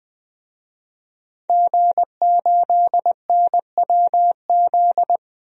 Morse code, heard as a keyed tone, about 20 wpm, 700 Hz. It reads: G8NWZ
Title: DXCC-100 Call Signs - 5 Characters - 20wpm